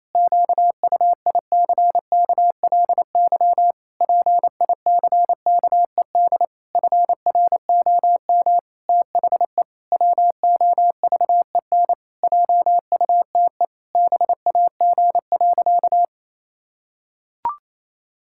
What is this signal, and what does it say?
Morse code, heard as a keyed tone, about 28 wpm, 700 Hz. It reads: QUICKLY PICKED FROM THE WOVEN JUTE BAG.
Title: Pangram Challenge v2 - 28wpm